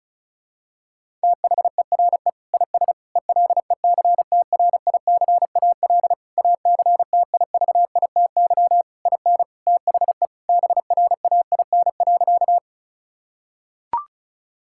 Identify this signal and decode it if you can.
Morse code, heard as a keyed tone, about 36 wpm, 700 Hz. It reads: THERE IS ELECTRICAL ACTIVITY IN THE BRAIN.
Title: Sentences from Top 1000 Words - Review Part 6 - 35wpm